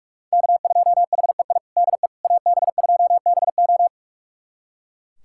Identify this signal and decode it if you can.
Morse code, heard as a keyed tone, about 41 wpm, 700 Hz. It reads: K2HEI DE AB2BO
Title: USA Callsign DE Callsign - 5 Characters - 45wpm